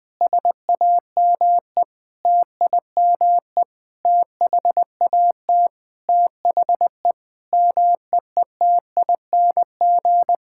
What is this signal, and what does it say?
Morse code, heard as a keyed tone, about 21 wpm, 700 Hz. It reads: SAME TIME THAT THE MEETING
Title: Sentences with 5-Word Phrases - 20wpm